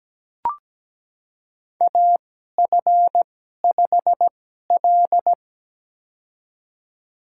Morse code, heard as a keyed tone, about 17 wpm, spaced as for 12 wpm, 700 Hz. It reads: AF5L
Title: USA Call Signs - 4 Characters - 2x Character Spacing - 17wpm